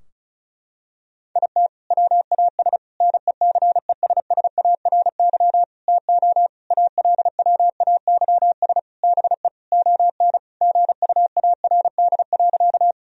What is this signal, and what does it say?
Morse code, heard as a keyed tone, about 36 wpm, 700 Hz. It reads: IT WAS NECESSARY TO ALWAYS BE ON GUARD.
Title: Sentences from Top 1000 Words - Review Part 4 - 35wpm